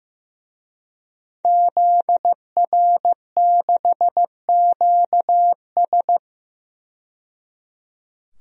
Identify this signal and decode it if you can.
Morse code, heard as a keyed tone, about 15 wpm, 700 Hz. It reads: ZR6QS
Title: DXCC-100 Call Signs - 5 Characters - 15wpm